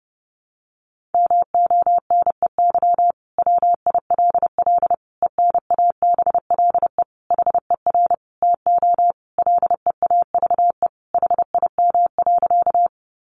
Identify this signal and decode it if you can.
Morse code, heard as a keyed tone, about 30 wpm, 700 Hz. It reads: MONEY WILL ENABLE HER TO LEAVE HIM.
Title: Sentences from Top 1500 Words 30wpm